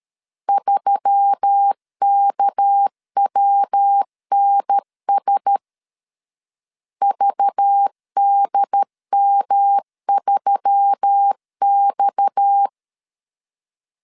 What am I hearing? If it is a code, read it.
Morse code, heard as a keyed tone, about 13 wpm, 800 Hz. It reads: 3KWNS VDM3X
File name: LEZIONI / 8 YJ45 / 8 YJ 12wpm 2235